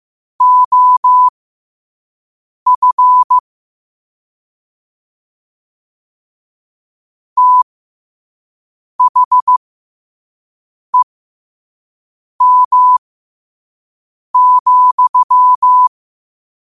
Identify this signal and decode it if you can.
Morse code, heard as a keyed tone, about 15 wpm, spaced as for 5 wpm, 1000 Hz. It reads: OF THEM,
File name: qotdfiles / 5wpm